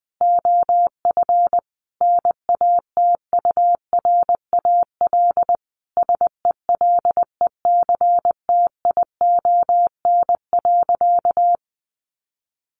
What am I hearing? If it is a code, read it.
Morse code, heard as a keyed tone, about 20 wpm, 700 Hz. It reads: OF NATURAL SELECTION.
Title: Sentences from Top 1400 Words 20wpm